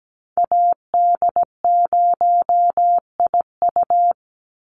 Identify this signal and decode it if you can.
Morse code, heard as a keyed tone, about 17 wpm, 700 Hz. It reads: AD0IU
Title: Call Signs - 17wpm